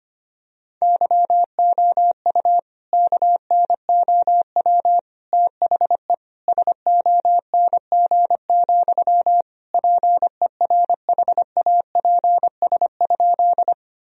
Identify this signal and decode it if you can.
Morse code, heard as a keyed tone, about 25 wpm, 700 Hz. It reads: YOU KNOW THE SONG, PERHAPS?
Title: Sentences from Top 1000 Words - Review Part 2 - 25wpm